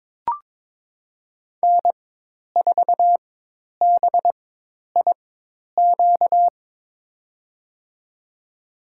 Morse code, heard as a keyed tone, about 22 wpm, spaced as for 10 wpm, 700 Hz. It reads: N4BIQ
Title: USA Call Signs - 5 Characters - 4x Character Spacing - 22wpm